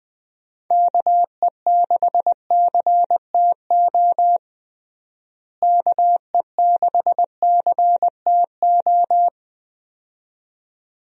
Morse code, heard as a keyed tone, about 21 wpm, 700 Hz. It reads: KE6CTO KE6CTO
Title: USA Call Signs - 6 Characters - Repeated 2x - 20wpm